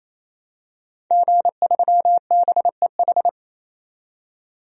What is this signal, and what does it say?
Morse code, heard as a keyed tone, about 28 wpm, 700 Hz. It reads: G3BEH
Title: DXCC-100 Call Signs - 5 Characters - 28wpm